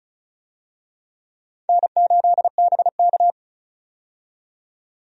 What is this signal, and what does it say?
Morse code, heard as a keyed tone, about 34 wpm, 700 Hz. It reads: N8BK
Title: USA Call Signs - 4 Characters - 35wpm